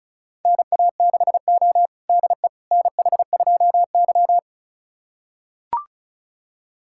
Morse code, heard as a keyed tone, about 36 wpm, 700 Hz. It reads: NA6O DE NH2Y
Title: USA Callsign DE Callsign - 4 Characters - 35wpm